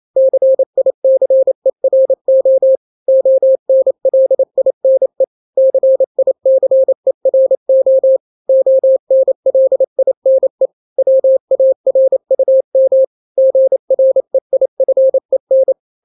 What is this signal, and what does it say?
Morse code, heard as a keyed tone, about 28 wpm, 530 Hz. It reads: CICERO ONLINE CICERO ONLINE WARUM GREIFEN
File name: daily / 27wpm